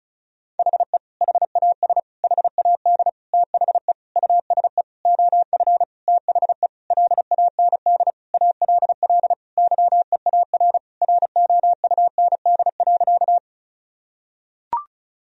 Morse code, heard as a keyed tone, about 34 wpm, 700 Hz. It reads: HE HAS HAD THE USE OF THE LAND ALL YEAR ROUND.
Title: Sentences from Top 1000 Words - Review Part 1 - 35wpm